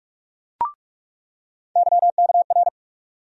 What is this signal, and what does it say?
Morse code, heard as a keyed tone, about 46 wpm, 700 Hz. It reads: YKR
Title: USA Call Sign Postfixes - 3 Letters 45wpm